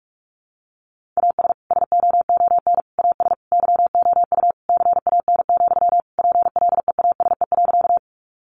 Morse code, heard as a keyed tone, about 44 wpm, 700 Hz. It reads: AS SOON AS YOU CAN, PLEASE.
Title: Sentences from Top 1500 Words 45wpm